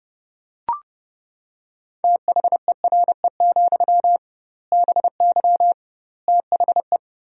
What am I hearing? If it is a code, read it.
Morse code, heard as a keyed tone, about 32 wpm, 700 Hz. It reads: THERE, BY THE
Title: 2x Word-Spacing - Sentences from Top 500 Words - 30wpm